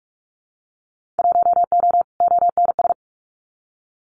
Morse code, heard as a keyed tone, about 46 wpm, 700 Hz. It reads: 1O ONS
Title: Winter Field Day Exchanges Encore 45wpm